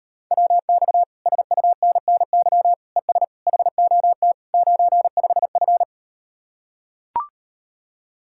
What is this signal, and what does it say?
Morse code, heard as a keyed tone, about 38 wpm, 700 Hz. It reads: WX SUNNY ES HOT 95F